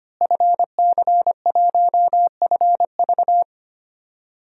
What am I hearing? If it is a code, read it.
Morse code, heard as a keyed tone, about 25 wpm, 700 Hz. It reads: FC1FV
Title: DXCC-100 Call Signs - 5 Characters - Repeated 3x - 25wpm